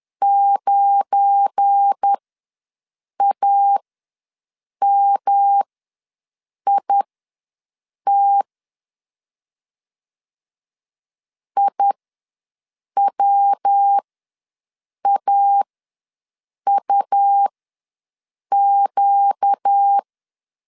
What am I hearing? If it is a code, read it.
Morse code, heard as a keyed tone, about 10 wpm, spaced as for 6 wpm, 800 Hz. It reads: 9AMIT IWAUQ
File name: LEZIONI / 7 GF9Z / 7 GF9 ws 10wpm 4034